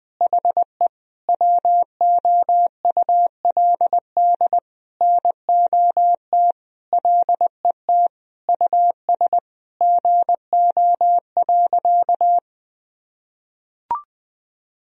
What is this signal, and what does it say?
Morse code, heard as a keyed tone, about 20 wpm, 700 Hz. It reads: HE WOULD NOT LET US GO.
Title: Sentences from Top 1000 Words - Review Part 2 - 20wpm